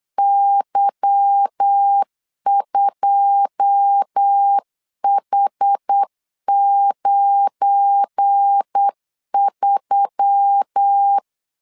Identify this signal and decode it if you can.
Morse code, heard as a keyed tone, about 8 wpm, 800 Hz. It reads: Y2H93
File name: CW / DET / 1 9 1500C S A 8wpm 5637